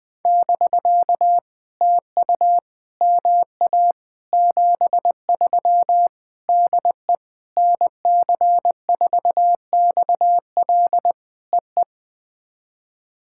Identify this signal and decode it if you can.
Morse code, heard as a keyed tone, about 20 wpm, 700 Hz. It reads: <BK> TU MA 73 DE NC4XL EE
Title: POTA QSO Practice - 20wpm